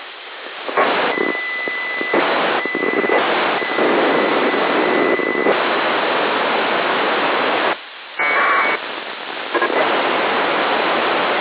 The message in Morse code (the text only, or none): none